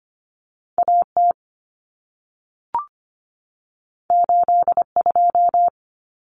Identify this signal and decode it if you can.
Morse code, heard as a keyed tone, about 25 wpm, 700 Hz. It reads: AT 82
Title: Common Cut Numbers - 2-Digit 25wpm